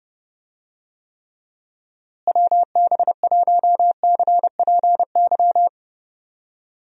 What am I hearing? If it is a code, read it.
Morse code, heard as a keyed tone, about 32 wpm, 700 Hz. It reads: WB1CPY
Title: USA Call Signs - 6 Characters - Repeated 2x - 30wpm